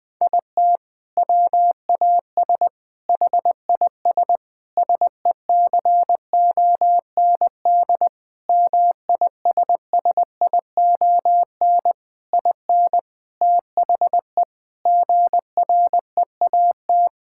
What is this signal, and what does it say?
Morse code, heard as a keyed tone, about 20 wpm, 700 Hz. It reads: IT WAS HIS SECOND MISSION IN THE GREAT
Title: Sentences from Top 1600 Words 20wpm